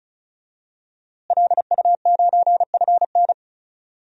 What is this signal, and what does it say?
Morse code, heard as a keyed tone, about 34 wpm, 700 Hz. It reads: LU9FN
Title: DXCC-100 Call Signs - 5 Characters - Repeated 3x - 35wpm